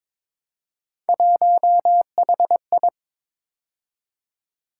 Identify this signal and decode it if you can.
Morse code, heard as a keyed tone, about 23 wpm, 700 Hz. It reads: 1HI
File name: practice / 22wpm / Three Letter-Number Combinations - 22wpm